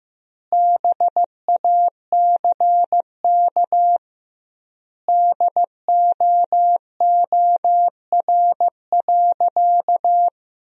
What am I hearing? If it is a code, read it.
Morse code, heard as a keyed tone, about 15 wpm, 700 Hz. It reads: BACK DOOR.
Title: 2x Word-Spacing - Sentences from Top 300 Words - 15wpm